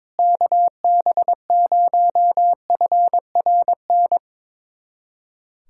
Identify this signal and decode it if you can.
Morse code, heard as a keyed tone, about 22 wpm, 700 Hz. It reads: KB0FRN
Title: USA Callsign DE Callsign - 6 Characters - 22wpm